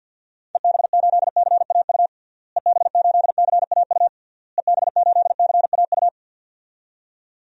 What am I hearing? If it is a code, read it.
Morse code, heard as a keyed tone, about 46 wpm, 700 Hz. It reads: EB8CAU EB8CAU EB8CAU
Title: DXCC-100 Call Signs - 6 Characters - Repeated 3x - 50wpm